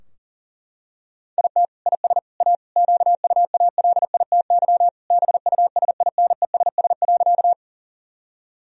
Morse code, heard as a keyed tone, about 40 wpm, 700 Hz. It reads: IT IS A QUALITY BUSINESS.
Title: Sentences from Top 1000 Words - Review Part 4 - 40wpm